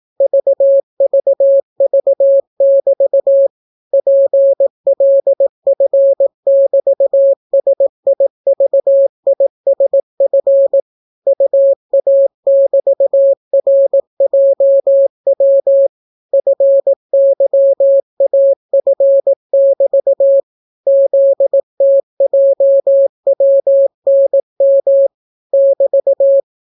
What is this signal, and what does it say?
Morse code, heard as a keyed tone, about 18 wpm, 550 Hz. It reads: VVV<BT> PLF<BT>SIVISF UA<BT>RJW FYAF<BT> ZTJWNM <BT>